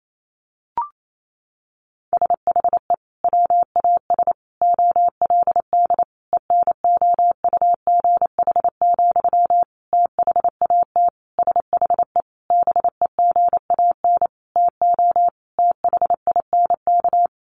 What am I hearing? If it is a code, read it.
Morse code, heard as a keyed tone, about 28 wpm, 700 Hz. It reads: SHE WAS OLD ENOUGH, THAT SHE BEGAN TO THINK
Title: Sentences from Top 800 Words 28wpm